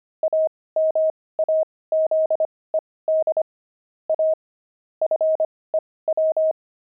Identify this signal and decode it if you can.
Morse code, heard as a keyed tone, about 25 wpm, spaced as for 18 wpm, 630 Hz. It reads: AMAZED A FEW
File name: pangram-fs-18-cs-25-antique